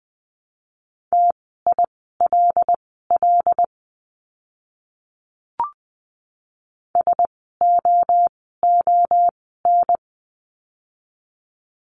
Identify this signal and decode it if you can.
Morse code, heard as a keyed tone, about 20 wpm, spaced as for 14 wpm, 700 Hz. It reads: TILL SOON
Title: 075 Lesson 13f - Letter L - Words 20wpm